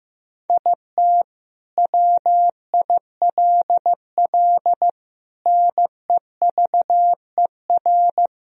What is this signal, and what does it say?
Morse code, heard as a keyed tone, about 15 wpm, 700 Hz. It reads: IT WILL NEVER
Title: Sentences from Top 1000 Words - Review Part 2 - 15wpm